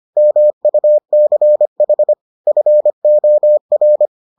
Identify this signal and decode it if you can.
Morse code, heard as a keyed tone, about 25 wpm, 600 Hz. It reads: MUCH FOR